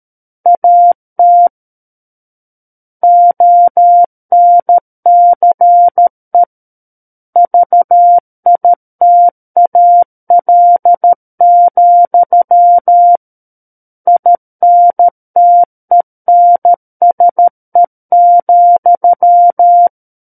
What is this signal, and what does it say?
Morse code, heard as a keyed tone, about 13 wpm, 700 Hz. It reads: AT ONCE VITAL, INTENSE,
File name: Book One / 04-Chapter 4 - 13 WPM